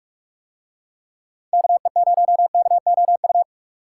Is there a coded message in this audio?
Yes: Morse code from 1.5 to 3.4 s, heard as a keyed tone, about 42 wpm, 700 Hz. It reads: KE0KOU